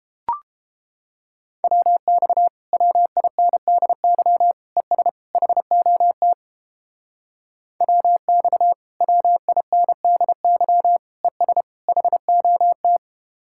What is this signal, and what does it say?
Morse code, heard as a keyed tone, about 31 wpm, 700 Hz. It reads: WX WINDY ES HOT WX WINDY ES HOT